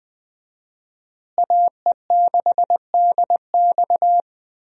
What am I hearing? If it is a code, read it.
Morse code, heard as a keyed tone, about 20 wpm, 700 Hz. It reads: AE6DX
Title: USA Call Signs - 5 Characters - Repeated 3x - 20wpm